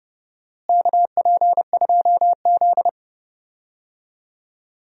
Morse code, heard as a keyed tone, about 32 wpm, 700 Hz. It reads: KP2Z
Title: DXCC-100 Call Signs - 4 Characters - Repeated 3x - 30wpm